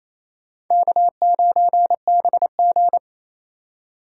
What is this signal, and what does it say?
Morse code, heard as a keyed tone, about 35 wpm, 700 Hz. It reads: K9BG
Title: Speed Racing - USA Call Signs 28wpm